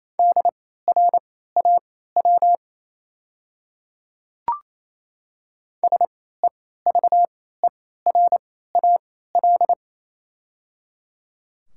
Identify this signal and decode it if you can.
Morse code, heard as a keyed tone, about 28 wpm, spaced as for 16 wpm, 700 Hz. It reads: DRAW SEVERAL
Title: 3x Character Spacing - Top 500 Words - 28wpm